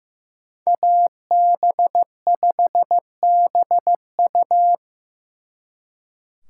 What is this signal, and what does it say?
Morse code, heard as a keyed tone, about 18 wpm, 700 Hz. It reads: AB5BU
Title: Speed Racing - USA 2x2 Call Signs 15wpm